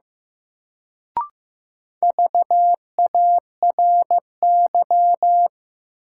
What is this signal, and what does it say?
Morse code, heard as a keyed tone, about 15 wpm, 700 Hz. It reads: VARY
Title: Sentences from Top 1200 Words - New Word Drill - 15wpm